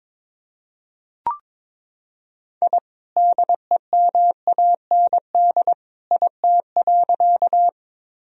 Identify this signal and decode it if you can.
Morse code, heard as a keyed tone, about 22 wpm, 700 Hz. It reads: I DEMAND IT.
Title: Sentences from Top 1000 Words - Review Part 4 - 22wpm